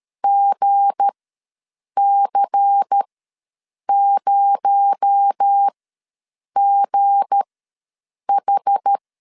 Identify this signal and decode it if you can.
Morse code, heard as a keyed tone, about 13 wpm, spaced as for 7 wpm, 800 Hz. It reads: GC0GH
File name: LEZIONI / 7 GF9Z / 7 G ws 12wpm 3140